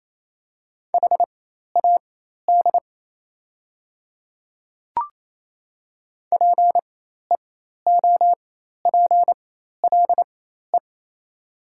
Morse code, heard as a keyed tone, about 28 wpm, spaced as for 13 wpm, 700 Hz. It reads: HAD PEOPLE
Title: 4x Character Spacing - Top 200 Words - 28wpm